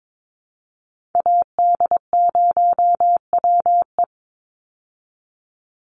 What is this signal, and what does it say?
Morse code, heard as a keyed tone, about 22 wpm, 700 Hz. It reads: AD0WE
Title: Call Signs Repeated 3x - 22wpm